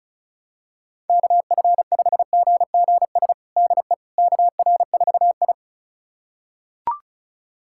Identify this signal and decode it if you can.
Morse code, heard as a keyed tone, about 36 wpm, 700 Hz. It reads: KF5GGS DE KR4I
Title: USA Callsign DE Callsign 35wpm